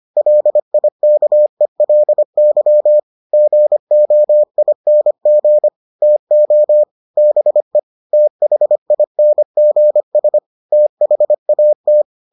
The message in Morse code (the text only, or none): LIKELY GOING TO BE THINGS THAT